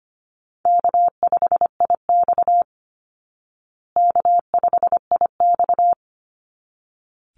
Morse code, heard as a keyed tone, about 25 wpm, 700 Hz. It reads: K5IX K5IX
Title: Call Signs Repeated 3x - 25wpm